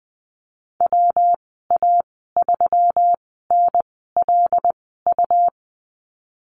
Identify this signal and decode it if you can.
Morse code, heard as a keyed tone, about 20 wpm, spaced as for 14 wpm, 700 Hz. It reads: WA3NLU
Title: 161 Lesson 26g - Letter g - Call Signs 20wpm